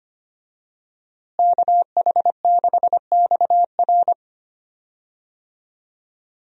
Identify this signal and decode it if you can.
Morse code, heard as a keyed tone, about 25 wpm, 700 Hz. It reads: KH6XR
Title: DXCC-100 Call Signs - 5 Characters - Repeated 3x - 25wpm